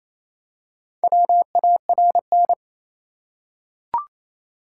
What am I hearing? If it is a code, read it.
Morse code, heard as a keyed tone, about 28 wpm, 700 Hz. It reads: WARN
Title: Four Letter Words - 28wpm